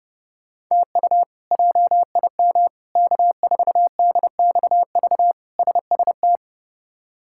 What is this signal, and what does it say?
Morse code, heard as a keyed tone, about 29 wpm, 700 Hz. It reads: TU JIM K4DXV SST
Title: K1USN SST QSO Practice - 30wpm